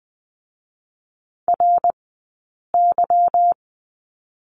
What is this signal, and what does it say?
Morse code, heard as a keyed tone, about 20 wpm, spaced as for 8 wpm, 700 Hz. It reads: RY
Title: 144 Lesson 24d - Letter y - Two Characters 20wpm